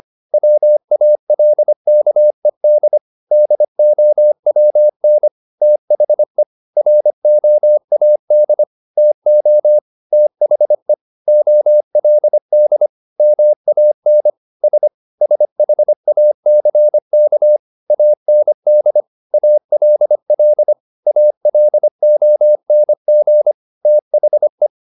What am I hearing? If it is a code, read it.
Morse code, heard as a keyed tone, about 25 wpm, 600 Hz. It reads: WALKED DOWN THE ROAD TO THE OLD MAN S SHACK AND ALL ALONG THE
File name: the old man and the sea part 1 25wpm